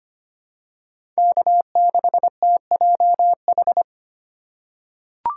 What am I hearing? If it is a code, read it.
Morse code, heard as a keyed tone, about 25 wpm, 700 Hz. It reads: K6TJH
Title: USA Call Signs - 5 Characters - 2x Character Spacing - 25wpm